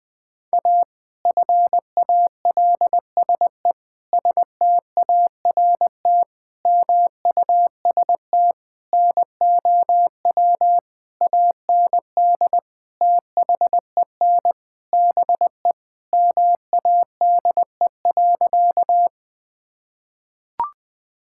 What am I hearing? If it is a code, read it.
Morse code, heard as a keyed tone, about 20 wpm, 700 Hz. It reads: A FALSE START MUST NOW AND THEN BE MADE.
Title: Sentences from Top 1600 Words 20wpm